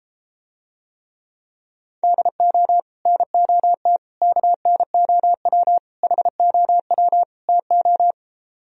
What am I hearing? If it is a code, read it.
Morse code, heard as a keyed tone, about 33 wpm, 700 Hz. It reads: DO NOT KNOW HOW TO